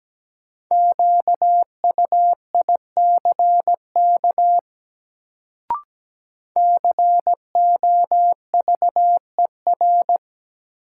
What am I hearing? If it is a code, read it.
Morse code, heard as a keyed tone, about 17 wpm, 700 Hz. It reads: QUICK COVER
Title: Sentences from Top 500 Words - New Word Drill - 17wpm